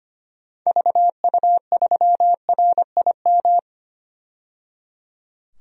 Morse code, heard as a keyed tone, about 25 wpm, 700 Hz. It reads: VU3RIM
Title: DXCC-100 Call Signs - 6 Characters - 25wpm